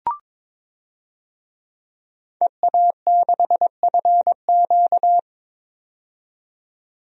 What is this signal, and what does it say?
Morse code, heard as a keyed tone, about 22 wpm, 700 Hz. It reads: EA6FQ
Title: DXCC-100 Call Signs - 5 Characters - 22wpm